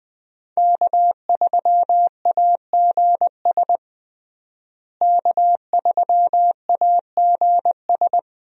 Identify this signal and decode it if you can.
Morse code, heard as a keyed tone, about 20 wpm, 700 Hz. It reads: K3AGS K3AGS
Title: USA Call Signs - 5 Characters - Repeated 3x - 20wpm